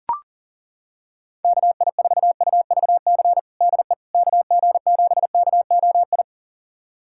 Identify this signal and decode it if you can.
Morse code, heard as a keyed tone, about 40 wpm, 700 Hz. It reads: KI4UVC DE KG7KOI
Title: USA Callsign DE Callsign - 6 Characters - 40wpm